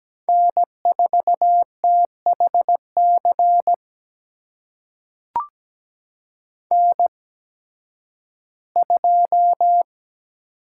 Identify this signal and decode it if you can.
Morse code, heard as a keyed tone, about 17 wpm, spaced as for 11 wpm, 700 Hz. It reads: N4THC N2
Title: USA Call Signs - 5 Characters - 8x Character Spacing - 17wpm